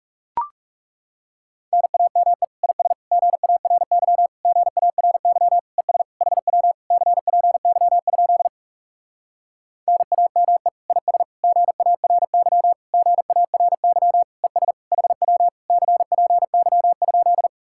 Morse code, heard as a keyed tone, about 40 wpm, 700 Hz. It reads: NAME IS GARY GARY ES HW CPY? NAME IS GARY GARY ES HW CPY?